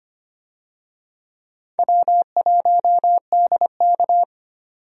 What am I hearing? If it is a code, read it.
Morse code, heard as a keyed tone, about 25 wpm, 700 Hz. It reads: W1DK
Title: USA Call Signs - 4 Characters - 25wpm